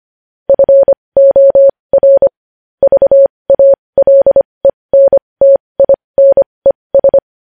Morse code, heard as a keyed tone, about 25 wpm, spaced as for 20 wpm, 550 Hz. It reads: FOR VALENTINES
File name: podcast / News headlines-1x-25x20